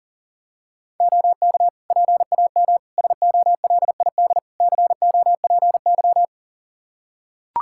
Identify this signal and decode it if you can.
Morse code, heard as a keyed tone, about 40 wpm, 700 Hz. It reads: OK PAM SOLID COPY
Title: K1USN SST QSO Practice 40wpm